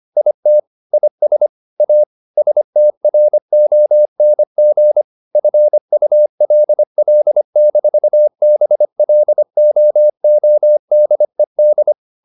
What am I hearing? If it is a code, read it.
Morse code, heard as a keyed tone, about 25 wpm, 600 Hz. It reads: IT IS A STRONG FULL-BLOODED
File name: the old man and the sea part 1 25wpm